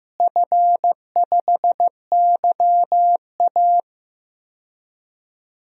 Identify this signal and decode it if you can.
Morse code, heard as a keyed tone, about 15 wpm, 700 Hz. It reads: F5YA
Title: DXCC-100 Call Signs - 4 Characters - Repeated 3x - 15wpm